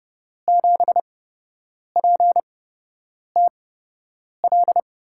Morse code, heard as a keyed tone, about 29 wpm, spaced as for 8 wpm, 700 Hz. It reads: 7PTL
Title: USA Call Signs - 5 Characters - 8x Character Spacing - 30wpm